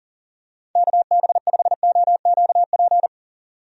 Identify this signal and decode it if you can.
Morse code, heard as a keyed tone, about 40 wpm, 700 Hz. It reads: KB5OQP
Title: USA Call Signs - 6 Characters - 40wpm